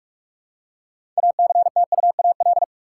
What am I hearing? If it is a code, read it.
Morse code, heard as a keyed tone, about 42 wpm, 700 Hz. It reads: ACTUAL
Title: Six Letter Words - 45wpm